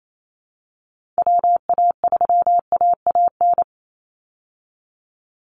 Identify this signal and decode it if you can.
Morse code, heard as a keyed tone, about 28 wpm, 700 Hz. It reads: WA3AAN
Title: Call Signs Repeated 3x - 28wpm